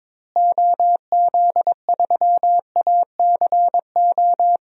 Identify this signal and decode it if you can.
Morse code, heard as a keyed tone, about 22 wpm, 700 Hz. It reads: OZ3ACO
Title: DXCC-100 Call Signs - 6 Characters - Repeated 3x - 22wpm